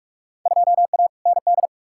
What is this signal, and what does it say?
Morse code, heard as a keyed tone, about 46 wpm, 700 Hz. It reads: 2A ND
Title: Field Day Exchanges - 45wpm